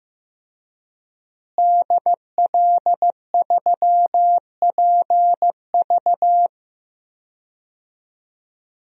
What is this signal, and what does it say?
Morse code, heard as a keyed tone, about 15 wpm, 700 Hz. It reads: DL3PV
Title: DXCC-100 Call Signs - 5 Characters - Repeated 2x - 15wpm